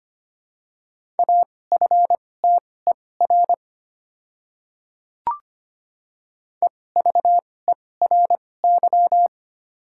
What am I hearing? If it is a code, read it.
Morse code, heard as a keyed tone, about 25 wpm, spaced as for 18 wpm, 700 Hz. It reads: AFTER EVERY
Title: 2x Character Spacing - Top 300 Words - 25wpm